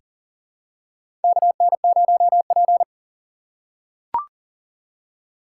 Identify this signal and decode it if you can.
Morse code, heard as a keyed tone, about 40 wpm, 700 Hz. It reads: KN0P
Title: USA Call Signs - 4 Characters - 4x Character Spacing - 40wpm